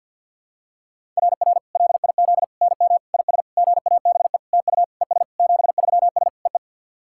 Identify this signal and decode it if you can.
Morse code, heard as a keyed tone, about 50 wpm, 700 Hz. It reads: RR LIZ NM IS GABE TU ES 73S EE
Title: SOTA QSO Practice 50wpm